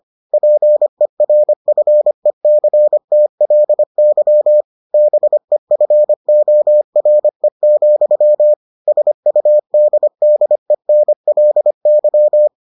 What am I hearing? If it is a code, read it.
Morse code, heard as a keyed tone, about 25 wpm, 600 Hz. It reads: PERFECTLY BEFORE, SUDDENLY